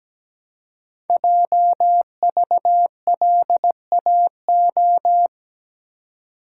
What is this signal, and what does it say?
Morse code, heard as a keyed tone, about 17 wpm, 700 Hz. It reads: JVLAO
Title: Code Groups - 5 Letter:Numbers - 17wpm